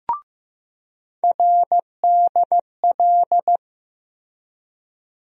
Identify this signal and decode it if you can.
Morse code, heard as a keyed tone, about 15 wpm, 700 Hz. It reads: RDL
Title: USA Call Sign Postfixes - 3 Letters 15wpm